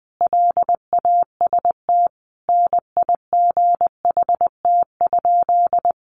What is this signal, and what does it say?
Morse code, heard as a keyed tone, about 20 wpm, 700 Hz. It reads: LAST NIGHT?
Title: Sentences from Top 600 Words 20wpm